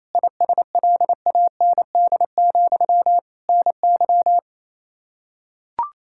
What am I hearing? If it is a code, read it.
Morse code, heard as a keyed tone, about 28 wpm, 700 Hz. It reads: ISLAND, NY
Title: Practice QSO - Exchange 1 - Beginning - 28wpm